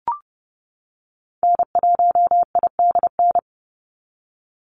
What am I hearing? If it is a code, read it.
Morse code, heard as a keyed tone, about 32 wpm, 700 Hz. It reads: N1IDN
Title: NA SOTA Activator Callsigns 30wpm